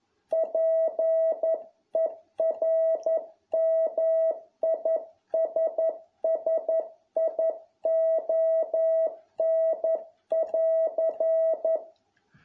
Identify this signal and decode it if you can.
Morse code, heard as a keyed tone, about 11 wpm, 650 Hz. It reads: PERMISSION<AR>